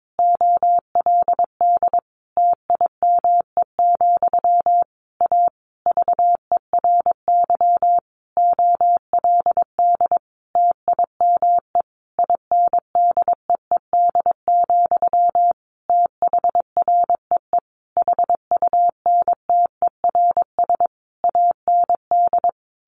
Morse code, heard as a keyed tone, about 22 wpm, 700 Hz. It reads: OLD TIME, A VERY OLD TIME INDEED, THREE HUNTERS AND
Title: A Sound Of Thunder 22wpm